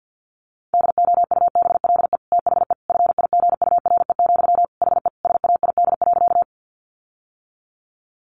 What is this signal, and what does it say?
Morse code, heard as a keyed tone, about 48 wpm, 700 Hz. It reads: DOUBLE THE FIGURE, HE SAID.
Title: Sentences from Top 1400 Words 50wpm